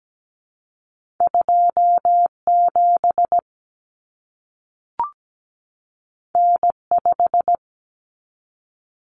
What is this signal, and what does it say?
Morse code, heard as a keyed tone, about 17 wpm, 700 Hz. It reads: 27 N5
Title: Common Cut Numbers - 2-Digit 17wpm